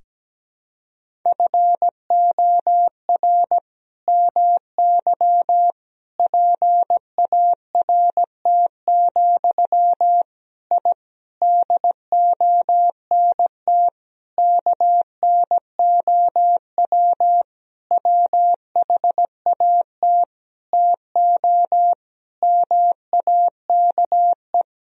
Morse code, heard as a keyed tone, about 17 wpm, 700 Hz. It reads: FOR MY PART, I DONT KNOW WHAT TO MAKE
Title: Sentences from Top 1000 Words - Review Part 1 - 17wpm